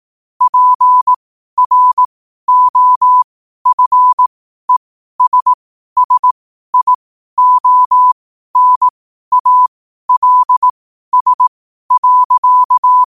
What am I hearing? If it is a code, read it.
Morse code, heard as a keyed tone, about 18 wpm, spaced as for 12 wpm, 1000 Hz. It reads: PROFESSIONALS.
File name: qotdfiles / 12wpm